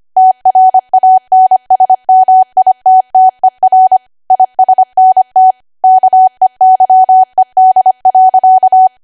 Morse code, heard as a keyed tone, about 25 wpm, 750 Hz. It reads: TRANSMITTER ISNT KEYED.
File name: Archive / 25 WPM / 25WPM